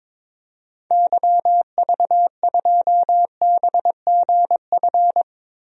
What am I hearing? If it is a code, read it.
Morse code, heard as a keyed tone, about 22 wpm, 700 Hz. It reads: YV2BGF
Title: DXCC-100 Call Signs - 6 Characters - Repeated 2x - 22wpm